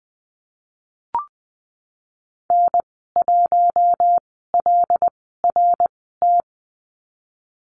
Morse code, heard as a keyed tone, about 20 wpm, spaced as for 15 wpm, 700 Hz. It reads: N1LRT
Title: 082 Lesson 14g - Number 2 - Call Signs 20wpm